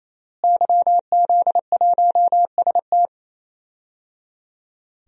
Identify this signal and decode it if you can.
Morse code, heard as a keyed tone, about 27 wpm, 700 Hz. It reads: YZ1ST
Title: DXCC-100 Call Signs - 5 Characters - Repeated 3x - 28wpm